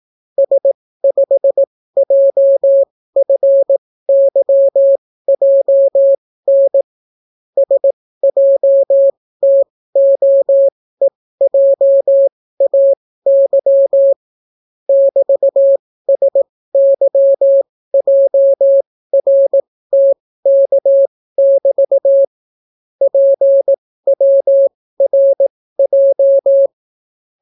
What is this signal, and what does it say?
Morse code, heard as a keyed tone, about 18 wpm, spaced as for 14 wpm, 550 Hz. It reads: S5JFYJN SJTOEJAY <BT>SYJRTK<BT> PWRJ